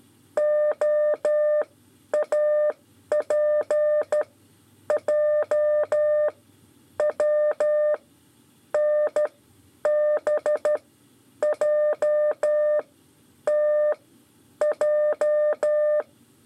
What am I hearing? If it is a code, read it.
Morse code, heard as a keyed tone, about 12 wpm, spaced as for 9 wpm, 600 Hz. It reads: OAPJW NBJTJ
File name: 5 LEZIONE RIPASSO 2